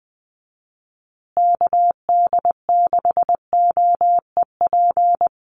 Morse code, heard as a keyed tone, about 20 wpm, 700 Hz. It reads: KD6OEP
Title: USA Call Signs Repeated 2x 20wpm